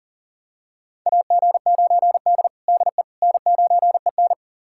Morse code, heard as a keyed tone, about 40 wpm, 700 Hz. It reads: AG9D DE N9EN
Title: USA Callsign DE Callsign - 4 Characters - 40wpm